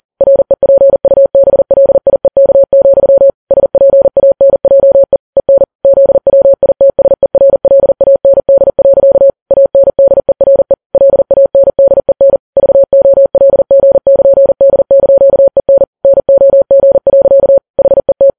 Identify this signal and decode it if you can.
Morse code, heard as a keyed tone, about 40 wpm, 550 Hz. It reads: REPUBLIEK, SPANJE EN ZWITSERLAND. ANDERE LANDEN VOLG<KN>D)EN NOG. HET
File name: Files Web tls / Morse on9fv / morse40kps